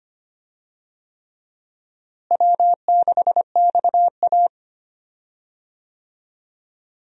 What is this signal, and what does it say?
Morse code, heard as a keyed tone, about 25 wpm, 700 Hz. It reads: W6XA
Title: USA Call Signs - 4 Characters - Repeated 3x - 25wpm